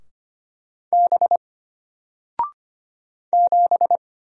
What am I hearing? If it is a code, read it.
Morse code, heard as a keyed tone, about 25 wpm, 700 Hz. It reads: B 7
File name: Single-Letter-Number-25wpm